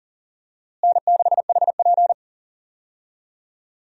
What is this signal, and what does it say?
Morse code, heard as a keyed tone, about 40 wpm, 700 Hz. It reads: N6HP
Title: USA Call Signs - 4 Characters - Repeated 2x - 40wpm